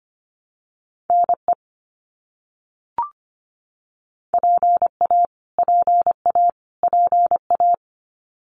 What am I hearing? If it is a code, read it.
Morse code, heard as a keyed tone, about 25 wpm, 700 Hz. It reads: NE PA PA PA
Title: US States Repeated 3x 25wpm